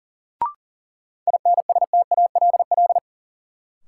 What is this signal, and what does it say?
Morse code, heard as a keyed tone, about 40 wpm, 700 Hz. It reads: INSTALL
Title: Sentences from Top 1600 Words - New Word Drill Alt - 40wpm